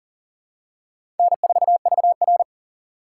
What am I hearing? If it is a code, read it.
Morse code, heard as a keyed tone, about 40 wpm, 700 Hz. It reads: N4VR
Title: USA Call Signs - 4 Characters - 2x Character Spacing - 40wpm